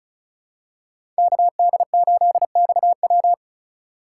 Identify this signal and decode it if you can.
Morse code, heard as a keyed tone, about 36 wpm, 700 Hz. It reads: KD8XW
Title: USA Call Signs - 5 Characters - 2x Character Spacing - 35wpm